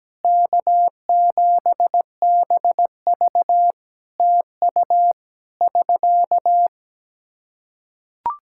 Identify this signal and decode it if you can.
Morse code, heard as a keyed tone, about 17 wpm, 700 Hz. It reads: K7BV TU <SK>
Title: K1USN SST QSO Practice 17wpm